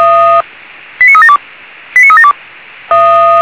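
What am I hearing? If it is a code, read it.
Five-tone selective call (ZVEI), heard as quick runs of five tones, tones of about 70 ms. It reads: 89372 89372
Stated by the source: ZVEI five-tone call received on an SDR, channel 464.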